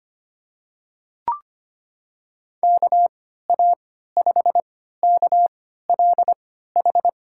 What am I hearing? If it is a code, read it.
Morse code, heard as a keyed tone, about 25 wpm, spaced as for 14 wpm, 700 Hz. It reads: KA5KLH
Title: USA Call Signs - 6 Characters - 3x Character Spacing - 25wpm